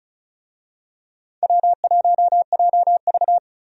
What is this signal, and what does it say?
Morse code, heard as a keyed tone, about 34 wpm, 700 Hz. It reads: W1JV
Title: USA Call Signs - 4 Characters - 35wpm